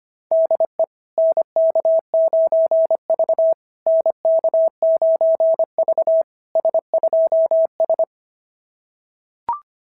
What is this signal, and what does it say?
Morse code, heard as a keyed tone, about 24 wpm, 650 Hz. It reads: DE NK9V NK9V S2S
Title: SOTA QSO Practice 25wpm